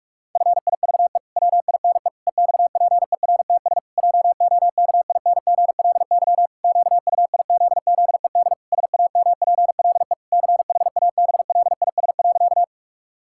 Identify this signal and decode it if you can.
Morse code, heard as a keyed tone, about 46 wpm, 700 Hz. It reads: FIVE WINE EXPERTS JOKINGLY QUIZZED SAMPLE CHABLIS.
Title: Pangram Challenge v2 - 45wpm